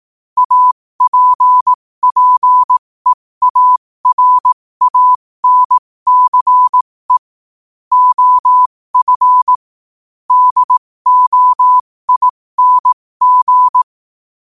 Morse code, heard as a keyed tone, about 18 wpm, spaced as for 15 wpm, 1000 Hz. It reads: APPEARANCE OF DOING
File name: qotdfiles / 15wpm